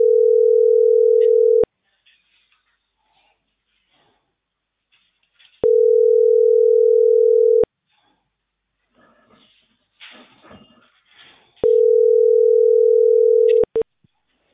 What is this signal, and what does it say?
Ringback tone (North America), 440 + 480 Hz.